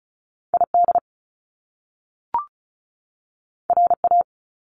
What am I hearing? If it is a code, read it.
Morse code, heard as a keyed tone, about 36 wpm, 700 Hz. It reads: ID RA
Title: Top Two Letter Combinations in Words 35wpm